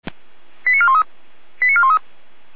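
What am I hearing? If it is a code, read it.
Five-tone selective call (ZVEI), heard as quick runs of five tones, tones of about 70 ms. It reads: 89413 89413
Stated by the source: Five-tone ZVEI selective call received on SDR.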